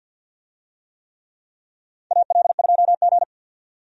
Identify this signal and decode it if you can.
Morse code, heard as a keyed tone, about 52 wpm, 700 Hz. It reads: AL2G
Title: USA Call Signs - 4 Characters - 50wpm